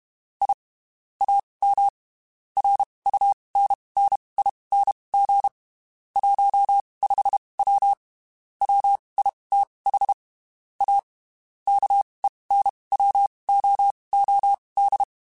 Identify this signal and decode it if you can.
Morse code, heard as a keyed tone, about 32 wpm, spaced as for 23 wpm, 800 Hz. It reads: I AM RUNNING 15W WITH A KENWOOD
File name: (spaced) / QSOs Spaced-mp3s / WD0BGP 1 @30 sp